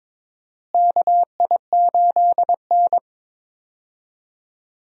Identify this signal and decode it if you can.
Morse code, heard as a keyed tone, about 22 wpm, 700 Hz. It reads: KI8N
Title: USA Call Signs - 4 Characters - Repeated 3x - 22wpm